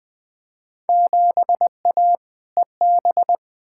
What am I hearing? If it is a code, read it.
Morse code, heard as a keyed tone, about 20 wpm, 700 Hz. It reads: 7A EB
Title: Field Day Exchanges Encore - 20wpm